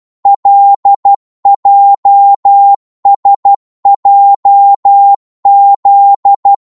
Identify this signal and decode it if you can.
Morse code, heard as a keyed tone, about 12 wpm, 800 Hz. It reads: LJSJZ
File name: Morse / ltr-12wpm